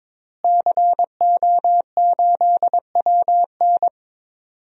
Morse code, heard as a keyed tone, about 21 wpm, 700 Hz. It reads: CO8WN
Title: DXCC-100 Call Signs - 5 Characters - Repeated 2x - 22wpm